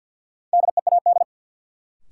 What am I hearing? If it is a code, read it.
Morse code, heard as a keyed tone, about 48 wpm, 700 Hz. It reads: DEAD
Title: Sentences from Top 1500 Words - New Word Drill - 50wpm